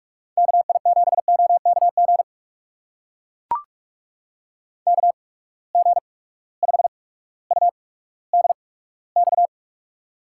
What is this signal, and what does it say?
Morse code, heard as a keyed tone, about 44 wpm, spaced as for 19 wpm, 700 Hz. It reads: KI7OKG KG5UDX
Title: USA Call Signs - 6 Characters - 8x Character Spacing - 45wpm